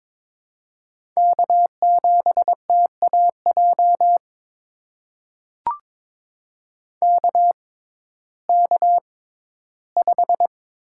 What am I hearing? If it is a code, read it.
Morse code, heard as a keyed tone, about 22 wpm, spaced as for 14 wpm, 700 Hz. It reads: K7TAJ KK5
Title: USA Call Signs - 5 Characters - 6x Character Spacing - 22wpm